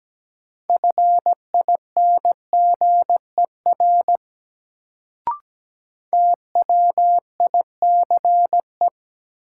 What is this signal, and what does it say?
Morse code, heard as a keyed tone, about 17 wpm, 700 Hz. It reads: FINGER TWICE
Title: Sentences from Top 1600 Words - New Word Drill Alt - 17wpm